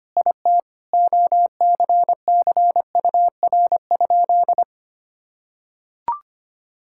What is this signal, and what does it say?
Morse code, heard as a keyed tone, about 25 wpm, 700 Hz. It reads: IT OCCUR?
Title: Sentences from Top 1000 Words - Review Part 6 - 25wpm